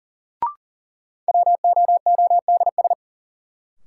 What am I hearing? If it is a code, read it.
Morse code, heard as a keyed tone, about 40 wpm, 700 Hz. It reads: WOODS
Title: Sentences from Top 1000 Words - Review Part 2 - New Word Drill - 40wpm